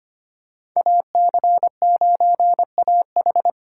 Morse code, heard as a keyed tone, about 25 wpm, 700 Hz. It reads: AC9AH
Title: USA Call Signs - 2x Character Spacing - 25wpm